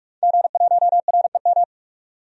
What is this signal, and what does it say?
Morse code, heard as a keyed tone, about 43 wpm, 700 Hz. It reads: G1REM
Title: DXCC-100 Call Signs - 5 Characters - 45wpm